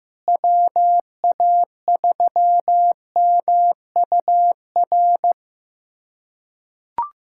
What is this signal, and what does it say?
Morse code, heard as a keyed tone, about 15 wpm, 700 Hz. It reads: WA3MUR
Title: USA Callsign DE Callsign - 6 Characters - 15wpm